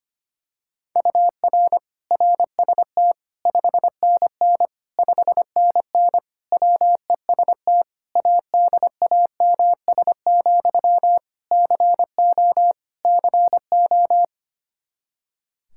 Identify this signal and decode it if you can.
Morse code, heard as a keyed tone, about 25 wpm, 700 Hz. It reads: UR RST 5NN 5NN WEST ADAMS, CO CO
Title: K1USN SST QSO Practice 25wpm